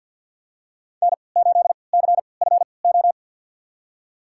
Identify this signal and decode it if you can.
Morse code, heard as a keyed tone, about 52 wpm, spaced as for 28 wpm, 700 Hz. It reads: N8CFO
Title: USA Call Signs - 5 Characters - 3x Character Spacing - 50wpm